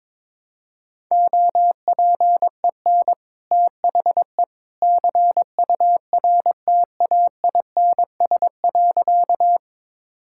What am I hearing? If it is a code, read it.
Morse code, heard as a keyed tone, about 22 wpm, 700 Hz. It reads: OPEN THE CURTAINS.
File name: practice / 22wpm / Easy Sentences with Hints - 22wpm